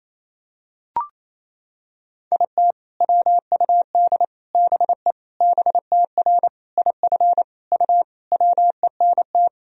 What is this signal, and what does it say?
Morse code, heard as a keyed tone, about 28 wpm, 700 Hz. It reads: IT WUD BE BTR IF U WENT
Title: Sentences from Top 400 Words and Abbreviations 28wpm